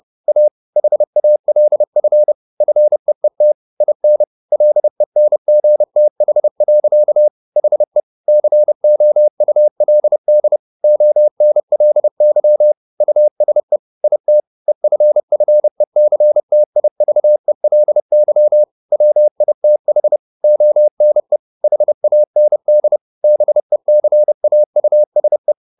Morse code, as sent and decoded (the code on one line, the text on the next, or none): .- / .... .- .-.. ..-. / ..-. . - / .. -. / .-.. -. --. - .... .-.-.- / .... / -.-. --- ..- .-.. -.. / --- -. .-.. -.-- / ..- ... / .. - / . ..-. ..-. -.-. - .. ...- . .-.. -.-- / .-- .. - .... / --- -. / .... .- -. -.. / -... -.-. .- ..- ... .
A HALF FEET IN LENGTH. HE COULD ONLY USE IT EFFECTIVELY WITH ONE HAND BECAUSE